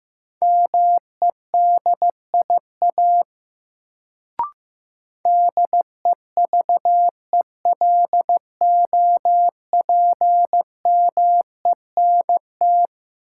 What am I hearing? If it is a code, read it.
Morse code, heard as a keyed tone, about 16 wpm, 700 Hz. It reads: MEDIA DEVELOPMENT
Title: Sentences from Top 1000 Words - Review Part 4 - New Word Drill - 15wpm